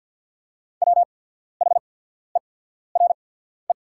Morse code, heard as a keyed tone, about 48 wpm, spaced as for 14 wpm, 700 Hz. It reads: WHERE
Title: 8x Character Spacing - Top 200 Words - 50wpm